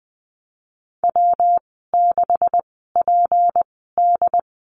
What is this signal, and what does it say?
Morse code, heard as a keyed tone, about 20 wpm, spaced as for 14 wpm, 700 Hz. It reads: W6PD
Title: 167 Lesson 27g - Number 7 - Call Signs 20wpm